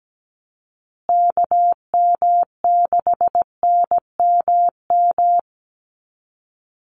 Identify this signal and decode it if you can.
Morse code, heard as a keyed tone, about 17 wpm, 700 Hz. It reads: KM6NMM
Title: USA Call Signs Repeated 2x 17wpm